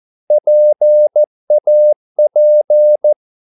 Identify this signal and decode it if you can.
Morse code, heard as a keyed tone, about 14 wpm, 600 Hz. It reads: PAP
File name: Korte ord DK 14wpm